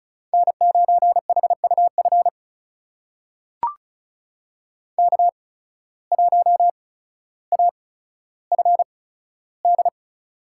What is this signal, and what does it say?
Morse code, heard as a keyed tone, about 34 wpm, spaced as for 15 wpm, 700 Hz. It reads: N9HUF K1AFD
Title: USA Call Signs - 5 Characters - 8x Character Spacing - 35wpm